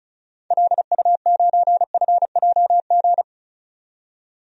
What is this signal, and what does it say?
Morse code, heard as a keyed tone, about 34 wpm, 700 Hz. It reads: LU9FJG
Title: DXCC-100 Call Signs - 6 Characters - 35wpm